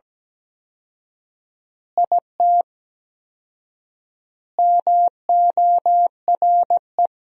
Morse code, heard as a keyed tone, about 17 wpm, 700 Hz. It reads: IT MORE
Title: 4x Word-Spacing - Sentences from Top 400 Words - 17wpm